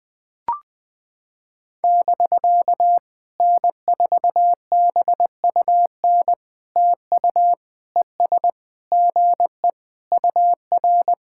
Morse code, heard as a keyed tone, about 20 wpm, 700 Hz. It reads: <BK> N4BUN TU ES GE UR
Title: POTA QSO Practice - 20wpm